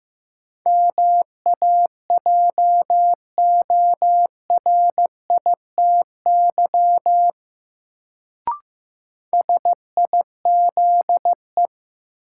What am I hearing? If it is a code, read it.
Morse code, heard as a keyed tone, about 15 wpm, 700 Hz. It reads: MAJORITY SIZE
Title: Sets of 3 Words - New Word Drill - 15wpm